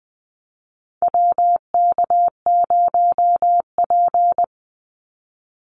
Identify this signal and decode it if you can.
Morse code, heard as a keyed tone, about 19 wpm, 700 Hz. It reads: WK0P
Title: Call Signs - 20wpm